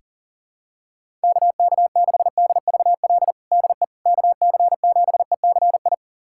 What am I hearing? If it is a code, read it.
Morse code, heard as a keyed tone, about 40 wpm, 700 Hz. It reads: KK6DVL DE KC7ECI
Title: USA Callsign DE Callsign - 6 Characters - 40wpm